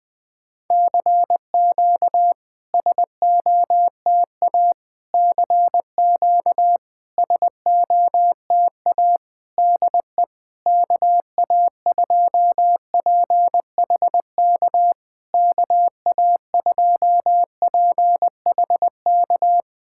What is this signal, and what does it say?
Morse code, heard as a keyed tone, about 21 wpm, 700 Hz. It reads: CQ SOTA CQ SOTA DE KA2PHK KA2PHK
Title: SOTA QSO Practice 20wpm